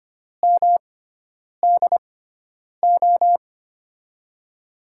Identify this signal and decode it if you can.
Morse code, heard as a keyed tone, about 25 wpm, spaced as for 9 wpm, 700 Hz. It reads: MDO
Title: USA Call Signs - 6 Characters - 6x Character Spacing - 25wpm